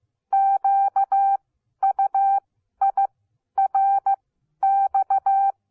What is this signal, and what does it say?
Morse code, heard as a keyed tone, about 15 wpm, spaced as for 11 wpm, 780 Hz. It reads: QUIRX
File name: Onlinelehrgang / morsen / DJ4UF Morsen14